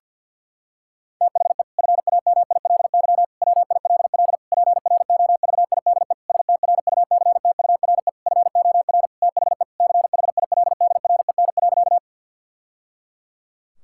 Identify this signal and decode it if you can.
Morse code, heard as a keyed tone, about 50 wpm, 700 Hz. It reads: THE FAMILY WILL PROVIDE STRUCTURE FOR THE CHILDREN.
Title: Sentences from Top 1000 Words - Review Part 6 - 50wpm